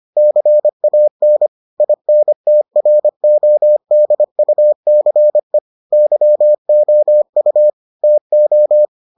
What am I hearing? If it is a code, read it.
Morse code, heard as a keyed tone, about 25 wpm, 600 Hz. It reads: CAN INTRODUCE YOU TO